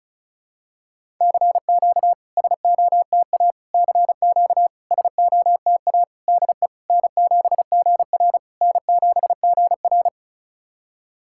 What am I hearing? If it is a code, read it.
Morse code, heard as a keyed tone, about 36 wpm, 690 Hz. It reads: CQ SOTA CQ SOTA DE N7GR N7GR
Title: SOTA QSO Practice 35wpm